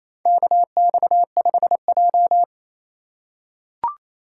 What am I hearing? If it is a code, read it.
Morse code, heard as a keyed tone, about 28 wpm, 700 Hz. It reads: KX5J
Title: USA Call Signs - 4 Characters - 3x Character Spacing - 28wpm